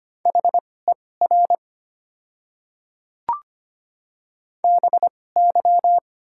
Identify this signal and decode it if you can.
Morse code, heard as a keyed tone, about 25 wpm, spaced as for 18 wpm, 700 Hz. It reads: HER BY
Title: 2x Character Spacing - Top 200 Words - 25wpm